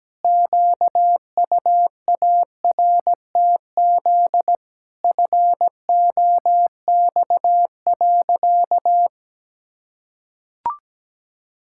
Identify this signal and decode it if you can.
Morse code, heard as a keyed tone, about 17 wpm, 700 Hz. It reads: QUARTZ FOX.
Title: Pangram Challenge v2 - 17wpm